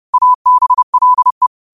Morse code, heard as a keyed tone, about 30 wpm, 1000 Hz. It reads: ABLE
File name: qotdfiles / 30wpm